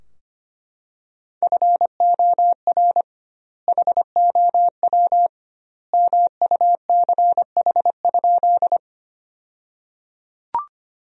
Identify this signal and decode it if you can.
Morse code, heard as a keyed tone, about 25 wpm, 700 Hz. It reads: FOR HOW MUCH?
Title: 2x Word-Spacing - Sentences from Top 200 Words - 25wpm